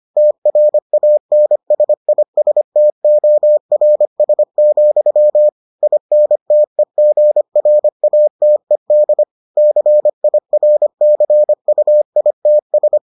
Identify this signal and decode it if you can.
Morse code, heard as a keyed tone, about 25 wpm, 600 Hz. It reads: TRANSISTORS, INTEGRATED CIRCUITS